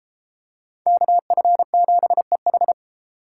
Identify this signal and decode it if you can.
Morse code, heard as a keyed tone, about 32 wpm, 700 Hz. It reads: KF7EH